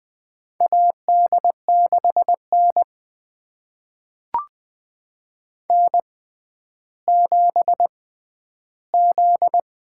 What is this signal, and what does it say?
Morse code, heard as a keyed tone, about 20 wpm, spaced as for 11 wpm, 700 Hz. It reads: AD6N N7Z
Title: USA Call Signs - 4 Characters - 6x Character Spacing - 20wpm